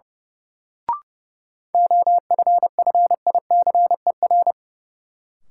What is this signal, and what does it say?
Morse code, heard as a keyed tone, about 30 wpm, 700 Hz. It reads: OFFICER
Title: Sentences from Top 1200 Words - New Word Drill Alt - 30wpm